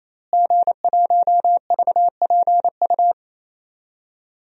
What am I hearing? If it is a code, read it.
Morse code, heard as a keyed tone, about 28 wpm, 700 Hz. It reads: G1VPU
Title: DXCC-100 Call Signs - 5 Characters - Repeated 3x - 28wpm